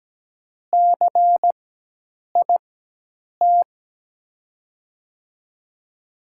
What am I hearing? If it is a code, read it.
Morse code, heard as a keyed tone, about 17 wpm, spaced as for 8 wpm, 700 Hz. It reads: CIT
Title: USA Call Signs - 5 Characters - 4x Character Spacing - 17wpm